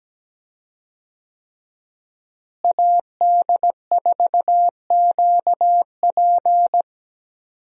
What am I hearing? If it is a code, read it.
Morse code, heard as a keyed tone, about 17 wpm, 700 Hz. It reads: AD4QP
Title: USA Call Signs - 5 Characters - Repeated 3x - 17wpm